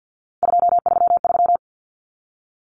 Morse code, heard as a keyed tone, about 48 wpm, 700 Hz. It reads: 233
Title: Numbers - 3-Digit 50wpm